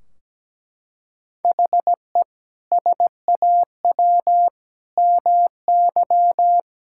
Morse code, heard as a keyed tone, about 17 wpm, 700 Hz. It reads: HE SAW MY
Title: Sentences from Top 1600 Words 17wpm